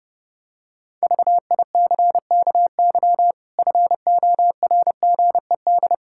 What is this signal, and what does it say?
Morse code, heard as a keyed tone, about 32 wpm, 700 Hz. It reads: VICKY FORGED
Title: Pangram Challenge v2 - 30wpm